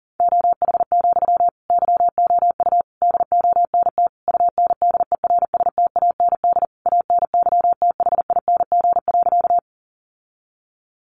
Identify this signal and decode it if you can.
Morse code, heard as a keyed tone, about 40 wpm, 700 Hz. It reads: OH, YOU DONT UNDERSTAND ANYTHING.
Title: Sentences from Top 1000 Words 40wpm